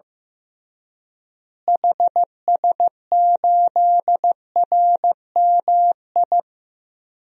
Morse code, heard as a keyed tone, about 15 wpm, 700 Hz. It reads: HS8RMI
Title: DXCC-100 Call Signs - 6 Characters - Repeated 3x - 15wpm